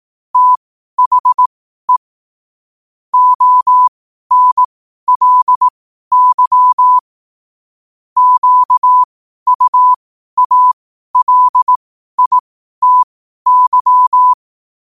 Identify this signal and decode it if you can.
Morse code, heard as a keyed tone, about 18 wpm, spaced as for 12 wpm, 1000 Hz. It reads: THE ONLY QUALITY
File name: qotdfiles / 12wpm